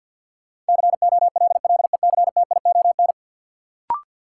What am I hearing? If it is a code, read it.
Morse code, heard as a keyed tone, about 53 wpm, 700 Hz. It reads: COLLECTION
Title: Sentences from Top 1200 Words - New Word Drill - 50wpm